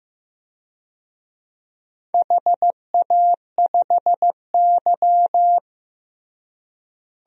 Morse code, heard as a keyed tone, about 16 wpm, 700 Hz. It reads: HA5Y
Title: DXCC-100 Call Signs - 4 Characters - Repeated 3x - 15wpm